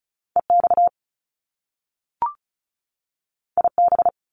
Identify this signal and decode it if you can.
Morse code, heard as a keyed tone, about 36 wpm, 700 Hz. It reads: EX IB
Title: Top Two Letter Combinations in Words 35wpm